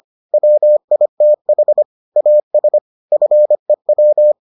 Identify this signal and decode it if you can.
Morse code, heard as a keyed tone, about 25 wpm, 600 Hz. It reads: WITH AS FEW